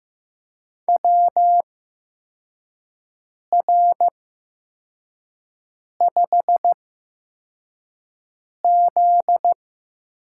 Morse code, heard as a keyed tone, about 15 wpm, spaced as for 4 wpm, 700 Hz. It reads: WR5Z
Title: USA Call Signs - 8x Character Spacing - 15wpm